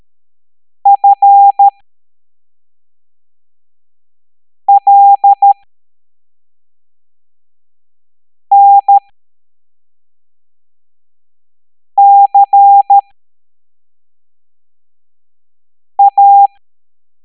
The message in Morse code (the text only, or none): FLNCA